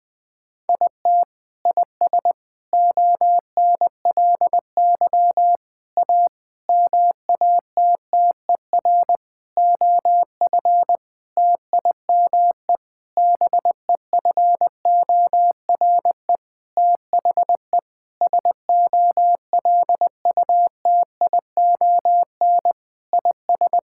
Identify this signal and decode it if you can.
Morse code, heard as a keyed tone, about 20 wpm, 700 Hz. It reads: IT IS ONLY A MATTER OF TIME BEFORE THE SOLUTION IS
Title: Sentences with 9-Word Phrases - 20wpm